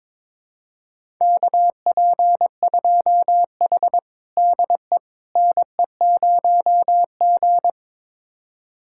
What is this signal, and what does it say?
Morse code, heard as a keyed tone, about 22 wpm, 700 Hz. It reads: KP2H DE NE0G
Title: USA Callsign DE Callsign - 4 Characters - 22wpm